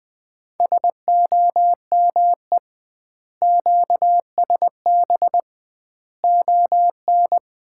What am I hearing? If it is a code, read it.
Morse code, heard as a keyed tone, about 20 wpm, 700 Hz. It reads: SOME QSB ON
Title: Common QSO Phrases - 2x Word-Spacing - 20wpm